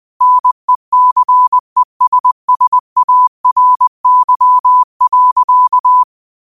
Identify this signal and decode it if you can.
Morse code, heard as a keyed tone, about 20 wpm, 1000 Hz. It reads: NECESSARY.